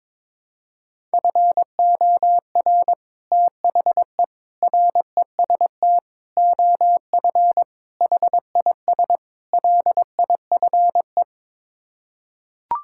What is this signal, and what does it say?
Morse code, heard as a keyed tone, about 21 wpm, 700 Hz. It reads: FOR THE REST OF HIS LIFE
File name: practice / 22wpm / Sets of 6 Words - 22wpm